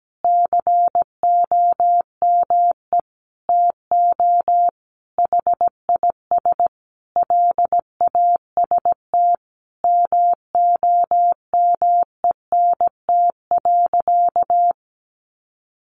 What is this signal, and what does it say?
Morse code, heard as a keyed tone, about 17 wpm, 700 Hz. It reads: COME TO HIS LAST MOMENT.
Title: Sentences from Top 1000 Words 17wpm